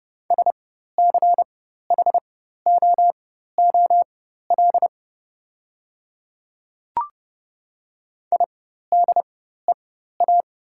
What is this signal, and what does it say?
Morse code, heard as a keyed tone, about 30 wpm, spaced as for 14 wpm, 700 Hz. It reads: SCHOOL IDEA
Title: 4x Character Spacing - Top 300 Words - 30wpm